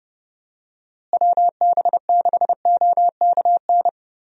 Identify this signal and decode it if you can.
Morse code, heard as a keyed tone, about 30 wpm, 700 Hz. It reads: WB6OKN
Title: USA Call Signs - 6 Characters - Repeated 3x - 30wpm